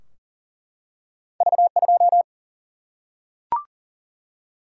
Morse code, heard as a keyed tone, about 40 wpm, 700 Hz. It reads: V 2
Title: Single-Letter Single-Number 40wpm